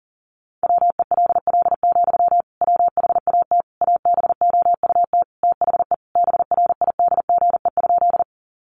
Morse code, heard as a keyed tone, about 40 wpm, 700 Hz. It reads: WELL, WHAT ABOUT THE BRIDGE?
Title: Sentences from Top 1200 Words 40wpm